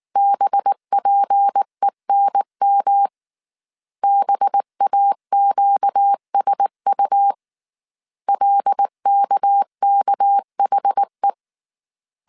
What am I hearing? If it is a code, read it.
Morse code, heard as a keyed tone, about 19 wpm, 800 Hz. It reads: BPENM BAQSU LKKHE
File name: CW / DET / 1 6 1500C S A 18wpm 2255